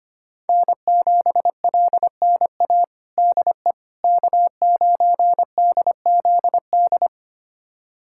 Morse code, heard as a keyed tone, about 25 wpm, 700 Hz. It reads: N7LNA DE K9DZD
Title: USA Callsign DE Callsign - 5 Characters - 25wpm